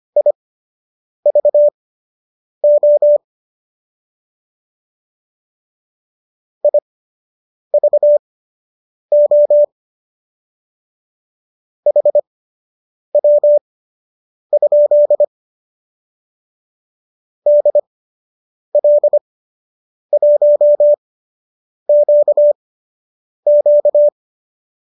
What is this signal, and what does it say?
Morse code, heard as a keyed tone, about 25 wpm, spaced as for 8 wpm, 590 Hz. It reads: IVO IVO HW? DL1QQ